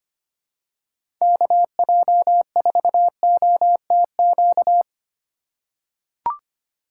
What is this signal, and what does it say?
Morse code, heard as a keyed tone, about 25 wpm, 700 Hz. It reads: KJ4OTQ
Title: USA Call Signs - 6 Characters - 6x Character Spacing - 25wpm